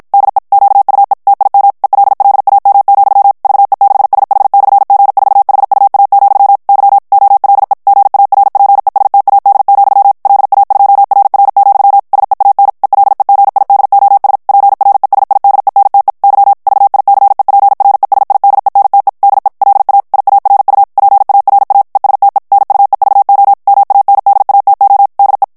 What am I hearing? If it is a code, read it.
Morse code, heard as a keyed tone, about 54 wpm, 800 Hz. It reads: DE QUE TIM ELLAM, VE6SH/G4HUA, Y OLE GARPESTAD, LA2RR, SEAN ELEGIDOS PRESIDENTE Y VICEPRESIDENTE DE LA IARU PARA ESTE NUEVO MANDATO DE
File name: ESPAGNOL TEXTE EN CLAIR 1 VITESSE 50 MOTS MINUTES